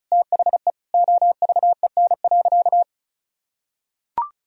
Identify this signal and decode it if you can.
Morse code, heard as a keyed tone, about 34 wpm, 700 Hz. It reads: THE OVEN.
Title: Sentences from Top 1000 Words - Review Part 4 - 35wpm